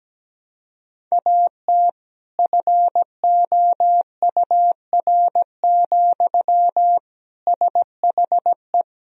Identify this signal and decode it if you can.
Morse code, heard as a keyed tone, about 17 wpm, 700 Hz. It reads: AT FOUR, SHE
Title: Sentences from Top 300 Words and Abbreviations 17wpm